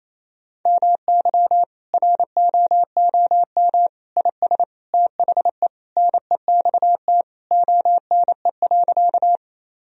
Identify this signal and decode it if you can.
Morse code, heard as a keyed tone, about 28 wpm, 700 Hz. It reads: MY ROOM IS THE NEXT ONE.
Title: Sentences from Top 1000 Words - Review Part 1 - 28wpm